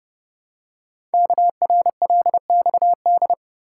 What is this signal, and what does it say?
Morse code, heard as a keyed tone, about 30 wpm, 700 Hz. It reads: KRLXD
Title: Code Groups - 5 Letters - 30wpm